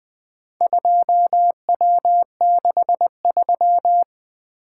Morse code, heard as a keyed tone, about 24 wpm, 700 Hz. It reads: 2W63
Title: Speed-Racing - Four Letter-Number Combinations - 20wpm